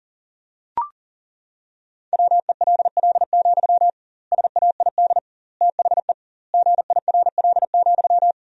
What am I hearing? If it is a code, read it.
Morse code, heard as a keyed tone, about 40 wpm, 700 Hz. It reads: WELL, SAID THE GIRL,
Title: 2x Word-Spacing - Sentences from Top 400 Words - 40wpm